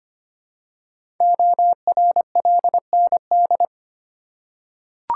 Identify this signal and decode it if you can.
Morse code, heard as a keyed tone, about 25 wpm, 700 Hz. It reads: ORLND
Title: Code Groups - 5 Letters - 25wpm